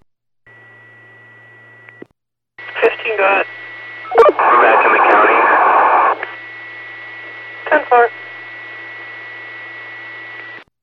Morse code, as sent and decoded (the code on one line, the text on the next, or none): none